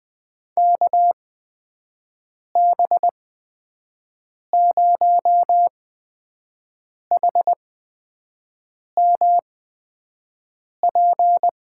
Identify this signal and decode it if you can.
Morse code, heard as a keyed tone, about 20 wpm, spaced as for 5 wpm, 700 Hz. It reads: KB0HMP
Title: USA Call Signs - 6 Characters - 8x Character Spacing - 20wpm